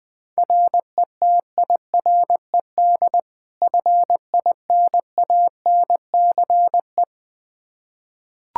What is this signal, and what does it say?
Morse code, heard as a keyed tone, about 20 wpm, 700 Hz. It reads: RETIRED FINANCE
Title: Practice QSO - Exchange 3 - More Info - 20wpm